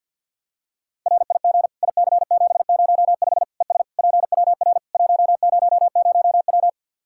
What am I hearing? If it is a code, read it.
Morse code, heard as a keyed tone, about 47 wpm, 700 Hz. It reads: RIG IC705 ES PWR 100W
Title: Practice QSO - Exchange 2 - Info - 50wpm